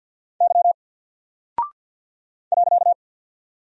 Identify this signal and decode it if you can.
Morse code, heard as a keyed tone, about 48 wpm, 700 Hz. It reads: Y .
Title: Single Character (Letter:Number:Punctuation:Prosign) 50wpm